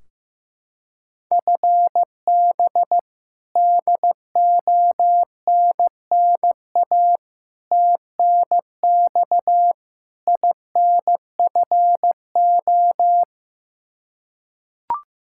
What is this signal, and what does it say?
Morse code, heard as a keyed tone, about 15 wpm, 700 Hz. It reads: FB DONNA TNX INFO
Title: Practice QSO - Exchange 3 - More Info - 15wpm